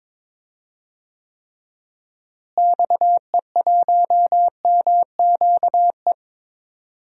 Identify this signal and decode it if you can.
Morse code, heard as a keyed tone, about 22 wpm, 700 Hz. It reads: XE1MQE
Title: DXCC-100 Call Signs - 6 Characters - Repeated 3x - 22wpm